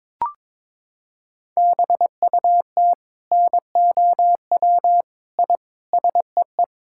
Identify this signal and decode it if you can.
Morse code, heard as a keyed tone, about 22 wpm, 700 Hz. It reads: BUT NOW I SEE
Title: Sentences from Top 1000 Words - Review Part 6 - 22wpm